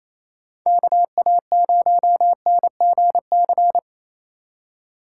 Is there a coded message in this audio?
Yes: Morse code from 0.7 to 3.8 s, heard as a keyed tone, about 37 wpm, 700 Hz. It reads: KA0NGC